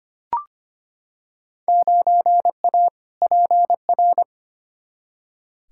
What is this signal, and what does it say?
Morse code, heard as a keyed tone, about 25 wpm, 700 Hz. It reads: 9A PR
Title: Field Day Exchanges Encore - 25wpm